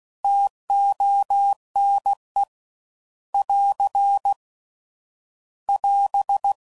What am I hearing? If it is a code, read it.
Morse code, heard as a keyed tone, about 16 wpm, 800 Hz. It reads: TONE <AR> <AS>